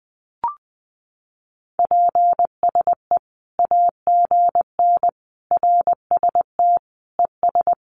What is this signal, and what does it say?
Morse code, heard as a keyed tone, about 20 wpm, 700 Hz. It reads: PSE AGN RST ES
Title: Common QSO Phrases 20wpm